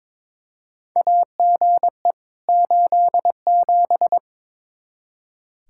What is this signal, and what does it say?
Morse code, heard as a keyed tone, about 22 wpm, 700 Hz. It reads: AGE 87
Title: K1USN SST QSO Practice 22wpm